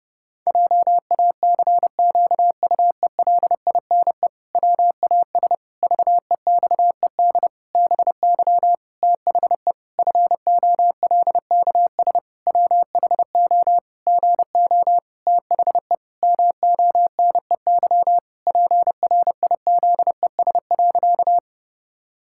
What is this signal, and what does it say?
Morse code, heard as a keyed tone, about 30 wpm, 700 Hz. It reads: JACQUELINE WAS VEXED BY THE FOLKS WHO GO THE MONEY PRIZES.
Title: Pangram Challenge v2 - 30wpm